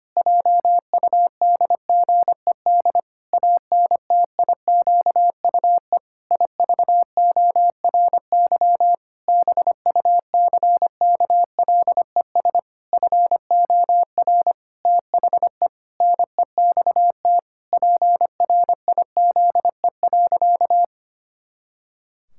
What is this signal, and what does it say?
Morse code, heard as a keyed tone, about 25 wpm, 700 Hz. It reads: JUDGED ANTIQUE IVORY BUCKLES FOR THE NEXT PRIZE.
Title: Pangram Challenge v2 - 25wpm